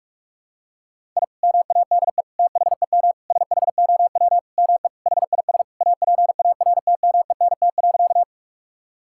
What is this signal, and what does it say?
Morse code, heard as a keyed tone, about 46 wpm, 700 Hz. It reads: I MADE THEM SHOW ME HIS APARTMENT.
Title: Sentences from Top 1000 Words - Review Part 6 - 45wpm